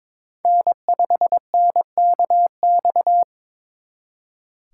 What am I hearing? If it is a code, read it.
Morse code, heard as a keyed tone, about 26 wpm, 700 Hz. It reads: N5NKX
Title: Speed Racing - USA 1x3 Call Signs 22wpm